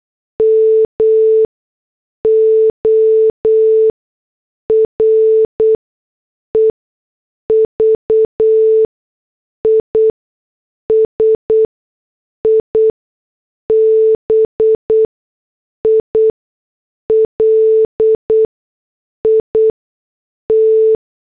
Morse code, heard as a keyed tone, about 8 wpm, spaced as for 6 wpm, 440 Hz. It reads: MORE VISIBILIT
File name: morse / headlines